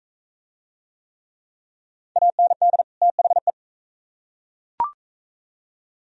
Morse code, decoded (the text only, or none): AND THE